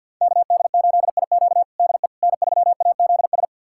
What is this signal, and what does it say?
Morse code, heard as a keyed tone, about 52 wpm, 700 Hz. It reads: KD8IQ DE N3AZS
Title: USA Callsign DE Callsign - 5 Characters - 50wpm